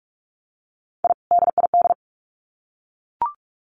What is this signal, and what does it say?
Morse code, heard as a keyed tone, about 44 wpm, 700 Hz. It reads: I DID
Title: Sets of 2 Words 45wpm